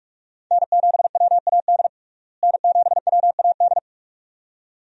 Morse code, heard as a keyed tone, about 41 wpm, 700 Hz. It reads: N7WAD N7WAD
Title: USA Call Signs - 5 Characters - Repeated 2x - 45wpm